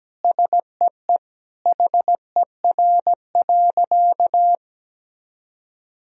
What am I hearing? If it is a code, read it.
Morse code, heard as a keyed tone, about 17 wpm, 700 Hz. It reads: SEE HER.
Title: Sentences from Top 1000 Words - Review Part 4 - 17wpm